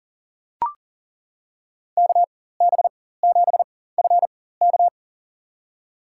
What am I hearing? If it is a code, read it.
Morse code, heard as a keyed tone, about 40 wpm, spaced as for 19 wpm, 700 Hz. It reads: KB7FK
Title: USA Call Signs - 5 Characters - 4x Character Spacing - 40wpm